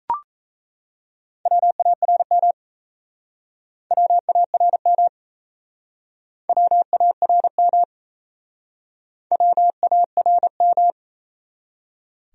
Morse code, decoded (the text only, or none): WARM WARM WARM WARM